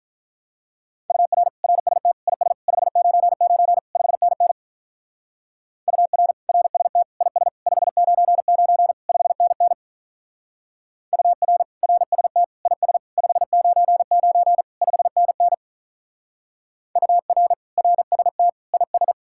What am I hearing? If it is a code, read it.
Morse code, heard as a keyed tone, about 43 wpm, 700 Hz. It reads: UR RST IS 599 5NN UR RST IS 599 5NN UR RST IS 599 5NN UR RST IS
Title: Speed-Racing Common QSO Phrases - 35wpm